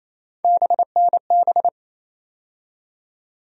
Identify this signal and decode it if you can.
Morse code, heard as a keyed tone, about 28 wpm, 700 Hz. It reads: BNB
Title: USA Call Sign Postfixes - 3 Letters 28wpm